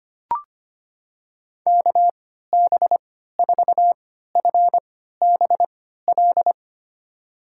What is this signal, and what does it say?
Morse code, heard as a keyed tone, about 25 wpm, spaced as for 14 wpm, 700 Hz. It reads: KB4FBL
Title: USA Call Signs - 3x Character Spacing - 25wpm